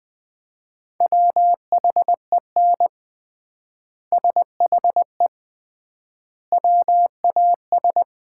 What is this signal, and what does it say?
Morse code, heard as a keyed tone, about 20 wpm, 700 Hz. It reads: WHEN SHE WAS
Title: 3x Word-Spacing - Sets of 3 Words - 20wpm